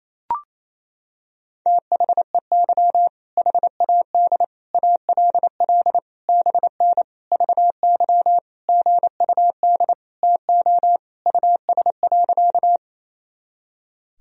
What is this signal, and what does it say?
Morse code, heard as a keyed tone, about 27 wpm, 700 Hz. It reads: THEY HAD ALL BN VY GUD TO US.
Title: Sentences from Top 200 Words and Abbreviations 28wpm